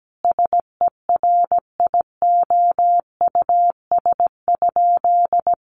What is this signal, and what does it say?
Morse code, heard as a keyed tone, about 17 wpm, 700 Hz. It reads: SERIOUS?
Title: Sentences from Top 1100 Words 17wpm